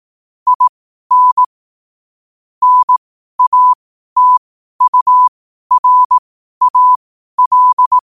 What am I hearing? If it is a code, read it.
Morse code, heard as a keyed tone, about 18 wpm, spaced as for 12 wpm, 1000 Hz. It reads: IN NATURAL
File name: qotdfiles / 12wpm